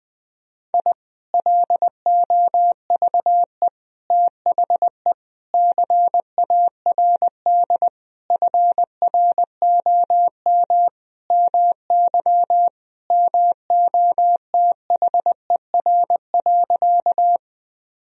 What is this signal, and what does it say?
Morse code, heard as a keyed tone, about 19 wpm, 700 Hz. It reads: I LOVE THE CARD FROM MY MOTHER.
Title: Sentences from Top 1000 Words - Review Part 4 - 20wpm